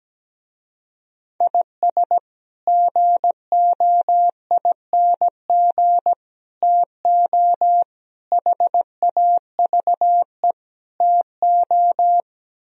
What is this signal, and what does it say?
Morse code, heard as a keyed tone, about 17 wpm, 700 Hz. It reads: IS GOING TO HAVE TO
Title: Speed-Racing - Sets of 5 Words 17wpm